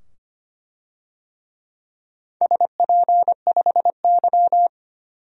Morse code, heard as a keyed tone, about 25 wpm, 700 Hz. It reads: SP5Y
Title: DXCC-100 Call Signs - 4 Characters - 25wpm